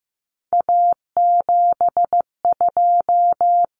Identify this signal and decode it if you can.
Morse code, heard as a keyed tone, about 15 wpm, 700 Hz. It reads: A72
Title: Common Cut Numbers - 3-Digit 15wpm